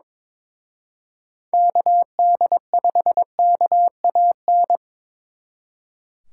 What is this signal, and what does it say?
Morse code, heard as a keyed tone, about 25 wpm, 700 Hz. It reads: KD5KAN
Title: Speed Racing - USA Call Signs 22wpm